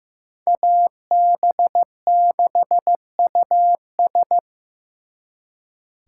Morse code, heard as a keyed tone, about 18 wpm, 700 Hz. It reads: AB6US
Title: Speed Racing - USA 2x2 Call Signs 15wpm